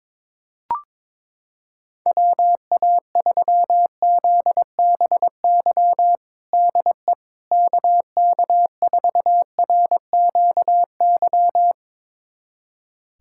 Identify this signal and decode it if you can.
Morse code, heard as a keyed tone, about 22 wpm, 700 Hz. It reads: WA3ZBY DE KK4RQY
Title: USA Callsign DE Callsign 22wpm